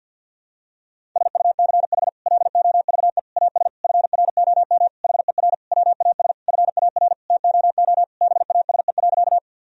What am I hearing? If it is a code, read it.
Morse code, heard as a keyed tone, about 51 wpm, 700 Hz. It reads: SUCH LOVE AS FROM HER WAS FAR TOO BASE.
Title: Sentences from Top 1000 Words - Review Part 1 - 50wpm